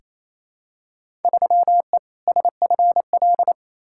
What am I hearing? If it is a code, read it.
Morse code, heard as a keyed tone, about 28 wpm, 700 Hz. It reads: 3E SFL
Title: Field Day Exchanges Encore - 28wpm